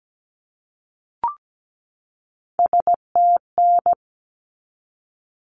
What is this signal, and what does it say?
Morse code, heard as a keyed tone, about 17 wpm, 700 Hz. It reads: STN
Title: Abbreviations 17wpm